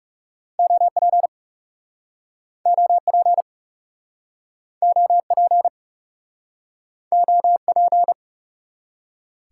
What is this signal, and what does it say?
Morse code, heard as a keyed tone, about 42 wpm, 700 Hz. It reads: OP OP OP OP